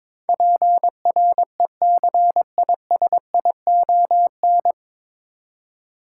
Morse code, heard as a keyed tone, about 22 wpm, 700 Hz. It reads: PRECISION
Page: Morse Code Ninja practice 22wpm